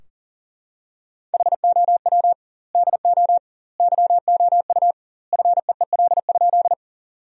Morse code, heard as a keyed tone, about 40 wpm, 700 Hz. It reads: HOW DO YOU FEEL?
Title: 2x Word-Spacing - Sentences from Top 400 Words - 40wpm